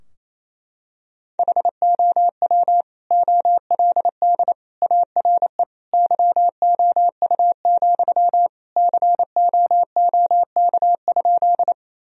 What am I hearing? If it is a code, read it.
Morse code, heard as a keyed tone, about 28 wpm, 700 Hz. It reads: HOW OLD ARE YOU, COOK?
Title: Sentences from Top 1000 Words - Review Part 4 - 28wpm